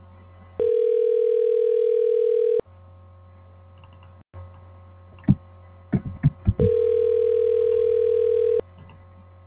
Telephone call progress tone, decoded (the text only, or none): ringback tone